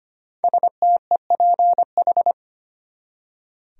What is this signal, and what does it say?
Morse code, heard as a keyed tone, about 25 wpm, 700 Hz. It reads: STEPH
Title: USA Common Names - 25wpm